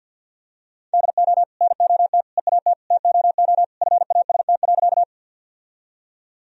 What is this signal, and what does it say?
Morse code, heard as a keyed tone, about 52 wpm, 700 Hz. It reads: DO NOT EAT TOO FAST.
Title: Sentences from Top 1000 Words - Review Part 2 - 50wpm